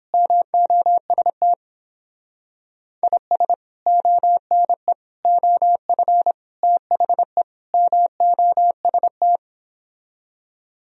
Morse code, heard as a keyed tone, about 29 wpm, 700 Hz. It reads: MOST IS ONE OF THE MOST